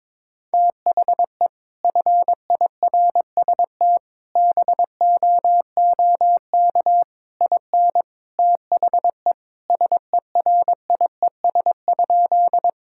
Morse code, heard as a keyed tone, about 22 wpm, 700 Hz. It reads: THE FIRST BOOK IN THE SERIES?
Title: Sentences from Top 1000 Words - Review Part 5 - 22wpm